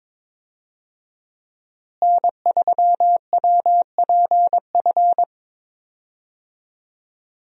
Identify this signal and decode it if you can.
Morse code, heard as a keyed tone, about 22 wpm, 700 Hz. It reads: N3WPF
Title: USA Call Signs - 5 Characters - Repeated 3x - 22wpm